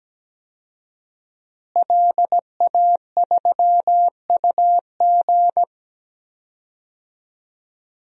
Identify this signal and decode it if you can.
Morse code, heard as a keyed tone, about 17 wpm, 700 Hz. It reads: LA3UG
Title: DXCC-100 Call Signs - 5 Characters - Repeated 2x - 17wpm